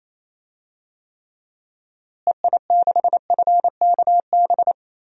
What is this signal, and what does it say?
Morse code, heard as a keyed tone, about 28 wpm, 700 Hz. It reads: EI6FKB
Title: DXCC-100 Call Signs - 6 Characters - Repeated 2x - 28wpm